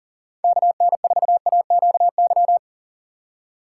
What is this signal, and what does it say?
Morse code, heard as a keyed tone, about 48 wpm, 700 Hz. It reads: KN4AQY
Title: Speed Racing - USA Call Signs 40wpm